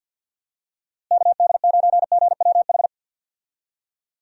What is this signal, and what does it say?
Morse code, heard as a keyed tone, about 46 wpm, 700 Hz. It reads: KD9GWH
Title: USA Call Signs - 6 Characters - Repeated 2x - 50wpm